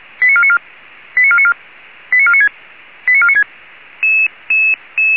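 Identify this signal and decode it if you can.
Five-tone selective call (ZVEI), heard as quick runs of five tones, tones of about 70 ms. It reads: 89494 89494 89496 89496
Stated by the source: ZVEI five-tone selective call 89155 received by SDR at feuerwehr-orlamuende (fire brigade Orlamünde).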